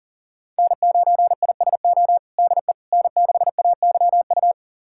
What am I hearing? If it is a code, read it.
Morse code, heard as a keyed tone, about 37 wpm, 700 Hz. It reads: N9ISO DE N6AYU
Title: USA Callsign DE Callsign - 5 Characters - 40wpm